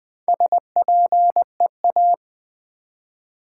Morse code, heard as a keyed tone, about 20 wpm, 700 Hz. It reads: SPEA
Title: Build A Word - 1 Letter At A Time - 5-Letter Words - 20wpm